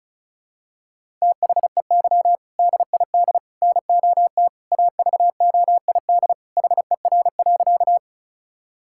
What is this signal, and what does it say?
Morse code, heard as a keyed tone, about 36 wpm, 700 Hz. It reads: THEY DID NOT AVOID HER.
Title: Sentences from Top 1000 Words - Review Part 6 - 35wpm